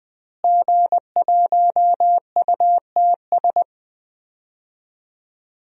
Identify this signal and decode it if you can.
Morse code, heard as a keyed tone, about 20 wpm, 700 Hz. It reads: G1UTS
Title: DXCC-100 Call Signs - 5 Characters - Repeated 2x - 20wpm